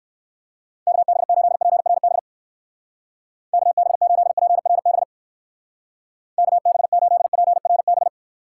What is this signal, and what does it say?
Morse code, heard as a keyed tone, about 60 wpm, 700 Hz. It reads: KB8PRB KB8PRB KB8PRB